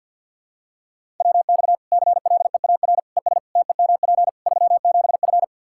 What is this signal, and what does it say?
Morse code, heard as a keyed tone, about 47 wpm, 700 Hz. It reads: WX CLEAR ES TEMP 37F
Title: K1USN SST QSO Practice 50wpm